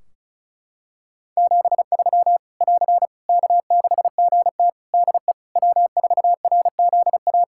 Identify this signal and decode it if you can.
Morse code, heard as a keyed tone, about 34 wpm, 700 Hz. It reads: 73 <AR> K6GT DE W4RZA
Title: Practice QSO - Exchange 4 - Ending - 35wpm